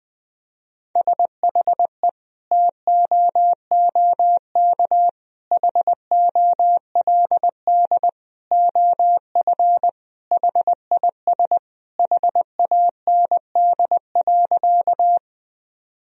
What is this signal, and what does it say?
Morse code, heard as a keyed tone, about 20 wpm, 700 Hz. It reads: SHE TOOK HOLD OF HIS HAND.
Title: Sentences from Top 1000 Words - Review Part 2 - 20wpm